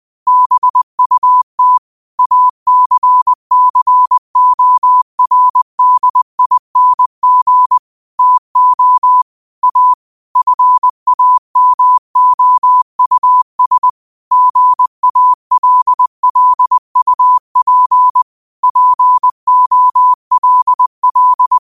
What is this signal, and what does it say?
Morse code, heard as a keyed tone, about 20 wpm, 1000 Hz. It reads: BUT ACCORDING TO A FAMOUS GALLUP POLL
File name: qotdfiles / 20wpm